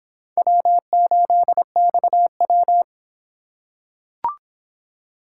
Morse code, heard as a keyed tone, about 23 wpm, 700 Hz. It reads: W8XW